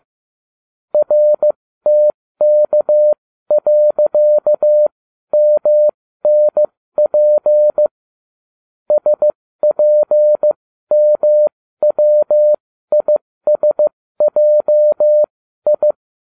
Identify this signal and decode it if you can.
Morse code, heard as a keyed tone, about 15 wpm, spaced as for 13 wpm, 600 Hz. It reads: RTK.MNP SPMWISJI